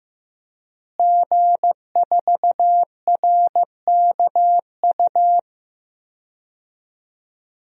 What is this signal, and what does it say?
Morse code, heard as a keyed tone, about 15 wpm, 700 Hz. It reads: G4RKU
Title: DXCC-100 Call Signs - 5 Characters - Repeated 2x - 15wpm